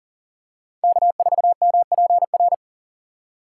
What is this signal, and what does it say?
Morse code, heard as a keyed tone, about 40 wpm, 700 Hz. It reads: K4MPR
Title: USA Call Signs - 5 Characters - 4x Character Spacing - 40wpm